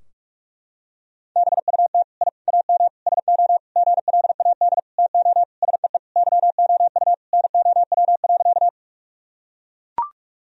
Morse code, heard as a keyed tone, about 41 wpm, 700 Hz. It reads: BUT I AM SO GLAD TO SEE YOU NOW.
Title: Sentences from Top 1000 Words - Review Part 2 - 45wpm